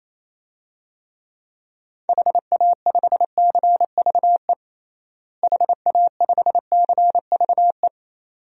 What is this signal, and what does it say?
Morse code, heard as a keyed tone, about 28 wpm, 700 Hz. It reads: HA5CVE HA5CVE
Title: DXCC-100 Call Signs - 6 Characters - Repeated 2x - 28wpm